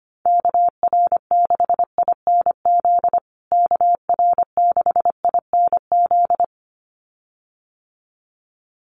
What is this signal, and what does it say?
Morse code, heard as a keyed tone, about 25 wpm, 700 Hz. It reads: KR6INZ KR6INZ
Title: USA Call Signs Repeated 2x 25wpm